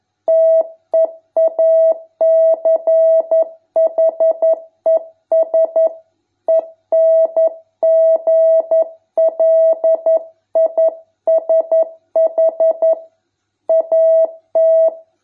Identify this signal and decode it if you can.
Morse code, heard as a keyed tone, about 11 wpm, 650 Hz. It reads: TEACHES ENGLISH AT